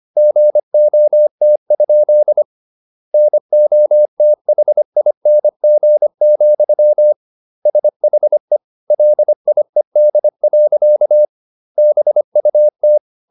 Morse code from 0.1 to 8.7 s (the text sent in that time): GOT? NOTHING, SHE